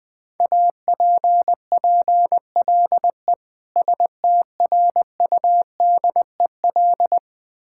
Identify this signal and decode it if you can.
Morse code, heard as a keyed tone, about 20 wpm, 700 Hz. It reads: APPLE STRUDEL
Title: Difficult Words with Hints - 20wpm